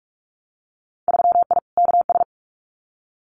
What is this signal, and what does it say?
Morse code, heard as a keyed tone, about 46 wpm, 700 Hz. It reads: 3I KS
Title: Winter Field Day Exchanges Encore 45wpm